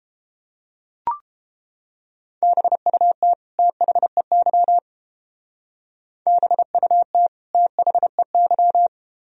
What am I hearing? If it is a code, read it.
Morse code, heard as a keyed tone, about 30 wpm, 700 Hz. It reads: BUT THEY BUT THEY